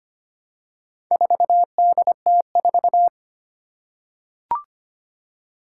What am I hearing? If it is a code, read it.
Morse code, heard as a keyed tone, about 25 wpm, 700 Hz. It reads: 4DT4
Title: Four Challenging Letter-Number Combinations 25wpm